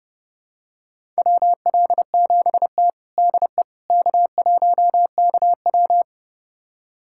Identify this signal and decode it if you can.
Morse code, heard as a keyed tone, about 30 wpm, 700 Hz. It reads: WL7T DE K1KW
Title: USA Callsign DE Callsign - 4 Characters - 30wpm